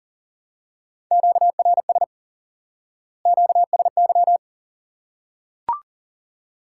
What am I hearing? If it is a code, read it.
Morse code, heard as a keyed tone, about 37 wpm, 700 Hz. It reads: QRS QSY
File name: IQE Course 3 - Lesson 89 - YAGI - Two Elements - 40wpm